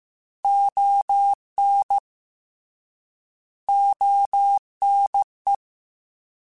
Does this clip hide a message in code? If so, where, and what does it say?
Yes: Morse code from 0.4 to 5.5 s, heard as a keyed tone, about 15 wpm, 800 Hz. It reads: ON ONE